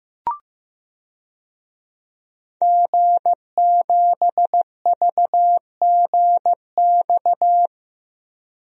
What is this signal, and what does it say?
Morse code, heard as a keyed tone, about 15 wpm, 700 Hz. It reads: G7VGX
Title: DXCC-100 Call Signs - 5 Characters - 15wpm